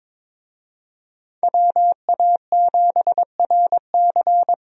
Morse code, heard as a keyed tone, about 22 wpm, 700 Hz. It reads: WA7RC
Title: USA Call Signs - 5 Characters - 22wpm